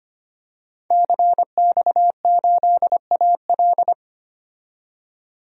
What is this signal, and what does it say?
Morse code, heard as a keyed tone, about 25 wpm, 700 Hz. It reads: CX8AL
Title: DXCC-100 Call Signs - 5 Characters - 25wpm